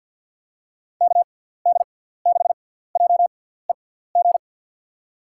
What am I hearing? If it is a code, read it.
Morse code, heard as a keyed tone, about 53 wpm, 700 Hz. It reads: K D 6 J E G
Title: USA Call Signs - 6 Characters - 6x Character Spacing - 50wpm